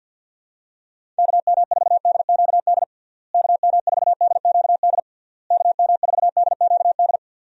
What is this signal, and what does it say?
Morse code, heard as a keyed tone, about 45 wpm, 700 Hz. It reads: KM4DQD KM4DQD KM4DQD
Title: USA Call Signs - 6 Characters - Repeated 3x - 50wpm